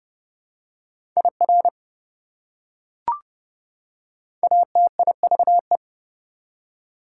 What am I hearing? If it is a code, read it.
Morse code, heard as a keyed tone, about 30 wpm, 700 Hz. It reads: IR ATIVE
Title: Prefixes and Suffixes 30wpm